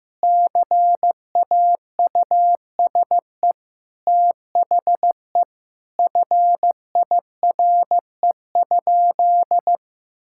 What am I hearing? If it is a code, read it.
Morse code, heard as a keyed tone, about 16 wpm, 700 Hz. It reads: CAUSE THE FIRE?
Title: Sentences from Top 1000 Words - Review Part 6 - 15wpm